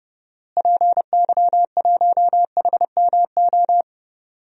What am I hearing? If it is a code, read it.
Morse code, heard as a keyed tone, about 30 wpm, 700 Hz. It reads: PY1HMO
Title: DXCC-100 Call Signs - 6 Characters - Repeated 2x - 30wpm